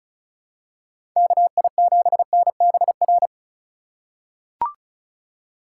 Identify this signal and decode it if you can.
Morse code, heard as a keyed tone, about 36 wpm, 700 Hz. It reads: KI7NBR
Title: USA Call Signs - 6 Characters - 8x Character Spacing - 35wpm